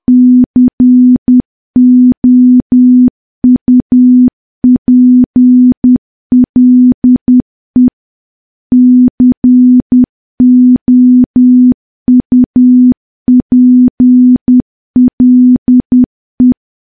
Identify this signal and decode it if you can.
Morse code, heard as a keyed tone, about 10 wpm, 250 Hz. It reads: COUPLE COUPLE
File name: morse / morse-training-10WPM